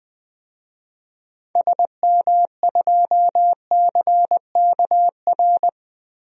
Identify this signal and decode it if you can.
Morse code, heard as a keyed tone, about 20 wpm, 700 Hz. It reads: SM2CKR
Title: DXCC-100 Call Signs - 6 Characters - 20wpm